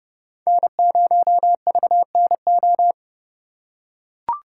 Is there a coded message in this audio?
Yes: Morse code from 0.5 to 2.9 s, heard as a keyed tone, about 31 wpm, 700 Hz. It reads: N0VNO